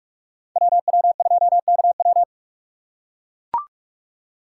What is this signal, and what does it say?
Morse code, heard as a keyed tone, about 44 wpm, 700 Hz. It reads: WW2KW